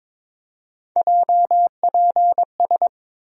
Morse code, heard as a keyed tone, about 22 wpm, 700 Hz. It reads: JPS
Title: USA Call Sign Postfixes - 3 Letters 22wpm